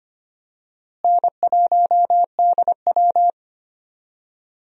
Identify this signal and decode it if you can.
Morse code, heard as a keyed tone, about 25 wpm, 700 Hz. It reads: N1DW
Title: USA Call Signs - 4 Characters - Repeated 2x - 25wpm